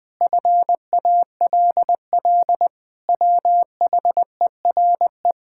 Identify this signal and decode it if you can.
Morse code, heard as a keyed tone, about 20 wpm, 700 Hz. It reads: FALL WHERE
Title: Sentences with 8-Word Phrases - 20wpm